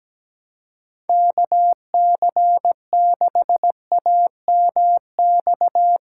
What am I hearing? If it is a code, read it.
Morse code, heard as a keyed tone, about 17 wpm, 700 Hz. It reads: KC6AMX
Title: USA Call Signs - 6x Character Spacing - 17wpm